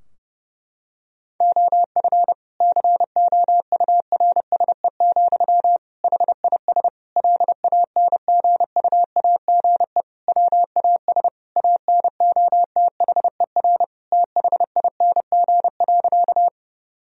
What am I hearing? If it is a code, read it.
Morse code, heard as a keyed tone, about 32 wpm, 700 Hz. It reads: OF COURSE, HIS LANGUAGE WAS ANOTHER THING.
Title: Sentences from Top 1000 Words - Review Part 3 - 30wpm